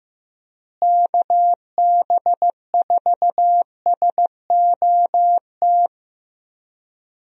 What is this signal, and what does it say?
Morse code, heard as a keyed tone, about 15 wpm, 700 Hz. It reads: KB4SOT
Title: USA Call Signs - 6 Characters - Repeated 2x - 15wpm